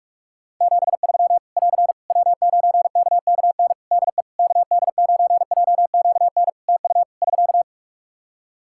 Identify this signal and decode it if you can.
Morse code, heard as a keyed tone, about 46 wpm, 700 Hz. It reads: 73 <AR> W9KKN DE KD9JQN TU <SK>
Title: Practice QSO - Exchange 4 - Ending - 45wpm